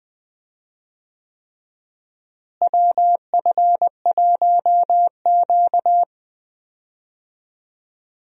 Morse code, heard as a keyed tone, about 21 wpm, 700 Hz. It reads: WF1Q
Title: USA Call Signs - 4 Characters - Repeated 3x - 20wpm